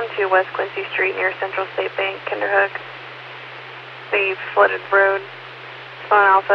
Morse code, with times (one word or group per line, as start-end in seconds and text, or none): none